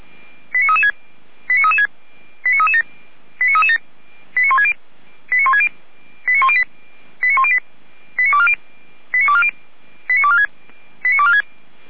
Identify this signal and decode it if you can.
Five-tone selective call (ZVEI), heard as quick runs of five tones, tones of about 70 ms. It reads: 89336 89337 89337 89160 89160 89108 89108 89240 89246 89246
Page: SDR/ZVEI